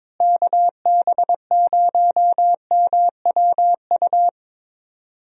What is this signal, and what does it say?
Morse code, heard as a keyed tone, about 22 wpm, 700 Hz. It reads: KB0MWU
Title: USA Call Signs - 6 Characters - Repeated 3x - 22wpm